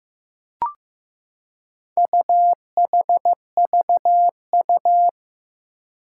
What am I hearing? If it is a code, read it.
Morse code, heard as a keyed tone, about 15 wpm, 700 Hz. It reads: UHVU
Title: Four Challenging Letter-Number Combinations 15wpm